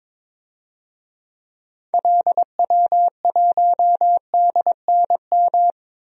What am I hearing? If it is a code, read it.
Morse code, heard as a keyed tone, about 22 wpm, 700 Hz. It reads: LW1DNM
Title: DXCC-100 Call Signs - 6 Characters - 22wpm